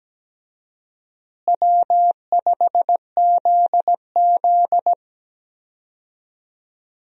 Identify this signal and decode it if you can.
Morse code, heard as a keyed tone, about 17 wpm, 700 Hz. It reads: W5ZZ
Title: USA Call Signs - 4 Characters - Repeated 3x - 17wpm